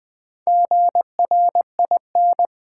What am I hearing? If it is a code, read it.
Morse code, heard as a keyed tone, about 20 wpm, 700 Hz. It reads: GRIN
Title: Build A Word - 2 Letters At A Time - 4-Letter Words - 20wpm